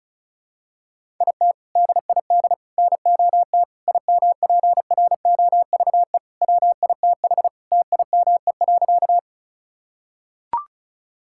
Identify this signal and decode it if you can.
Morse code, heard as a keyed tone, about 36 wpm, 700 Hz. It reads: IT DID NOT IMPROVE WITH TIME.
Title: Sentences from Top 1000 Words - Review Part 5 - 35wpm